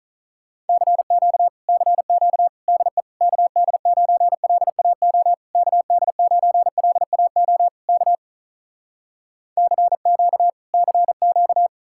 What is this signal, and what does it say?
Morse code, heard as a keyed tone, about 48 wpm, 700 Hz. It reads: CQ CQ DE KD9LAO KD9LAO K CQ CQ